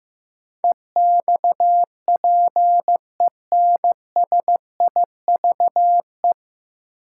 Morse code, heard as a keyed tone, about 15 wpm, 700 Hz. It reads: EXPENSIVE
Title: Sentences from Top 1500 Words - New Word Drill Alt - 15wpm